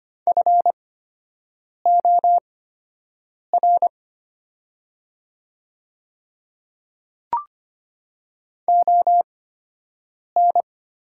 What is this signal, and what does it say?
Morse code, heard as a keyed tone, about 25 wpm, spaced as for 7 wpm, 700 Hz. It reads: FOR ON
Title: 8x Character Spacing - Top 100 Words - 25wpm